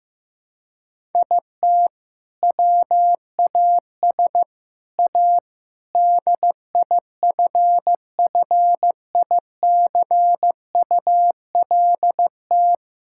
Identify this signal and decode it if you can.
Morse code, heard as a keyed tone, about 15 wpm, 700 Hz. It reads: IT WAS A DIFFICULT
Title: Sentences from Top 1000 Words - Review Part 6 - 15wpm